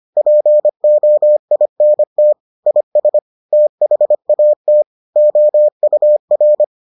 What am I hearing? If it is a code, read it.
Morse code, heard as a keyed tone, about 25 wpm, 600 Hz. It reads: POINT IS THAT OUR